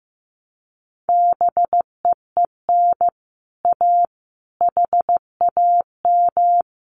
Morse code, heard as a keyed tone, about 15 wpm, 700 Hz. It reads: BEEN A HAM
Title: Common QSO Phrases Encore 15wpm